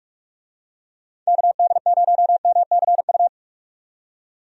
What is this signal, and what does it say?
Morse code, heard as a keyed tone, about 46 wpm, 700 Hz. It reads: KD0MCU
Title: USA Call Signs - 6 Characters - 45wpm